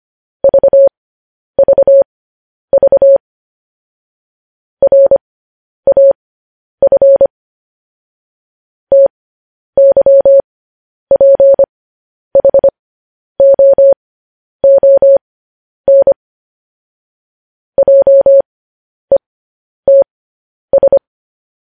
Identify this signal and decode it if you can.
Morse code, heard as a keyed tone, about 25 wpm, spaced as for 10 wpm, 550 Hz. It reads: VVV RAF TYPHOON JETS
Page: News Headlines in Morse Code at 10 WPM